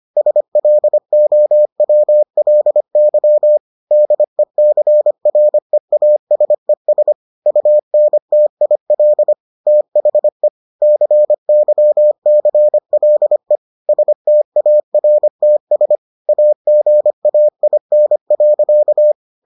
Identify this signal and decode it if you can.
Morse code, heard as a keyed tone, about 25 wpm, 600 Hz. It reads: SLOWLY DECREASES UNTIL THE CYCLE STARTS AGAIN.